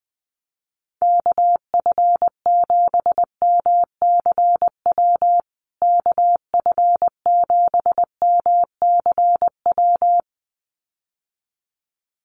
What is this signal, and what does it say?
Morse code, heard as a keyed tone, about 20 wpm, 700 Hz. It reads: KF7MCW KF7MCW
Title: USA Call Signs Repeated 2x 20wpm